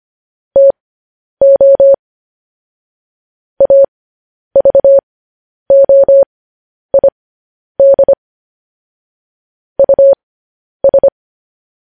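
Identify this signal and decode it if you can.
Morse code, heard as a keyed tone, about 25 wpm, spaced as for 10 wpm, 550 Hz. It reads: TO AVOID US
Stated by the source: Morse code news headlines sent at 10 wpm for practice.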